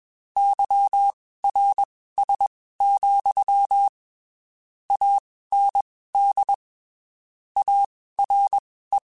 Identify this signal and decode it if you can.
Morse code, heard as a keyed tone, about 21 wpm, spaced as for 15 wpm, 800 Hz. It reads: YRS, AND ARE
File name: (spaced) / QSOs Spaced-mp3s / WD0BGP 1 @20 sp